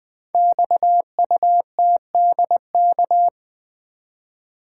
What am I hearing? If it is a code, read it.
Morse code, heard as a keyed tone, about 20 wpm, 700 Hz. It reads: XUTDK
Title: Code Groups - 5 Letters - 20wpm